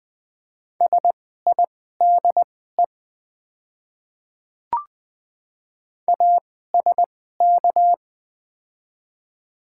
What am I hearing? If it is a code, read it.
Morse code, heard as a keyed tone, about 21 wpm, spaced as for 15 wpm, 700 Hz. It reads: SIDE ASK
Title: 2x Character Spacing - Top 300 Words - 20wpm